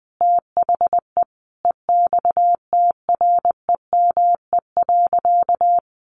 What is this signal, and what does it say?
Morse code, heard as a keyed tone, about 20 wpm, 700 Hz. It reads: THE EXTREME.
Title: Sentences from Top 1300 Words 20wpm